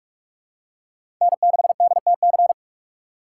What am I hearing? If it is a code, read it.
Morse code, heard as a keyed tone, about 46 wpm, 700 Hz. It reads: N6DTC
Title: USA Call Signs - 3x Character Spacing - 45wpm